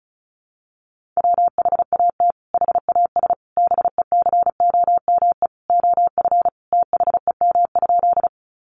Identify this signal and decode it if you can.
Morse code, heard as a keyed tone, about 34 wpm, 700 Hz. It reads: WHAT HAS BECOME OF THEM?
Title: Sentences from Top 600 Words 35wpm